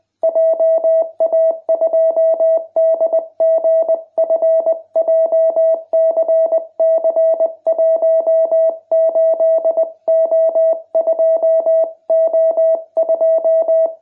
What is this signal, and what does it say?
Morse code, heard as a keyed tone, about 20 wpm, 660 Hz. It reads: JA2DGF JCC18O2O2